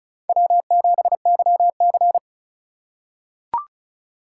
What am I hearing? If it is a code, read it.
Morse code, heard as a keyed tone, about 36 wpm, 700 Hz. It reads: W7YC
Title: USA Call Signs - 4 Characters - 2x Character Spacing - 35wpm